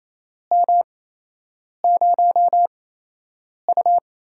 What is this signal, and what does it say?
Morse code, heard as a keyed tone, about 28 wpm, 700 Hz. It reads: M 0 U
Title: USA Call Signs - 4 Characters - 8x Character Spacing - 28wpm